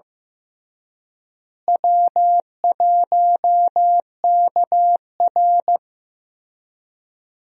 Morse code, heard as a keyed tone, about 15 wpm, 700 Hz. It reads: W1KR
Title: USA Call Signs - 4 Characters - Repeated 3x - 15wpm